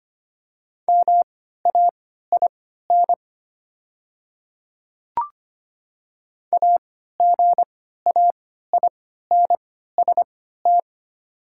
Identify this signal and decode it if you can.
Morse code, heard as a keyed tone, about 25 wpm, spaced as for 14 wpm, 700 Hz. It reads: MAIN AGAINST
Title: 3x Character Spacing - Top 500 Words - 25wpm